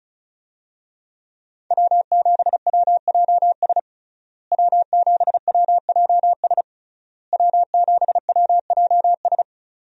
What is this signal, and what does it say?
Morse code, heard as a keyed tone, about 36 wpm, 700 Hz. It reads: W7WJS W7WJS W7WJS
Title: USA Call Signs - 5 Characters - Repeated 3x - 35wpm